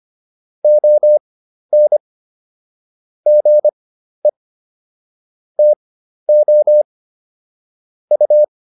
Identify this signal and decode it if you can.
Morse code, heard as a keyed tone, about 25 wpm, spaced as for 12 wpm, 600 Hz. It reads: ON GE TO U